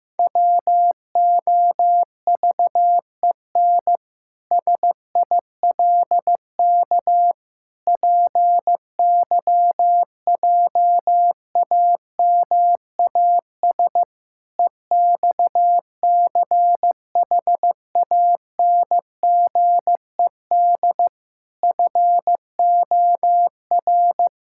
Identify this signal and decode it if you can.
Morse code, heard as a keyed tone, about 16 wpm, 700 Hz. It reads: WOVEN SILK PYJAMAS EXCHANGED FOR
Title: Pangram Challenge v2 - 15wpm